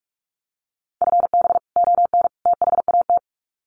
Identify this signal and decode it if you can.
Morse code, heard as a keyed tone, about 45 wpm, 700 Hz. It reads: FB ON THAT
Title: Common QSO Phrases 45wpm